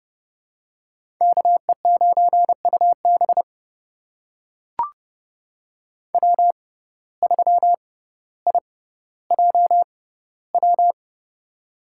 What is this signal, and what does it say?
Morse code, heard as a keyed tone, about 30 wpm, spaced as for 15 wpm, 700 Hz. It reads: KE9UB W3IJW
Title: USA Call Signs - 5 Characters - 6x Character Spacing - 30wpm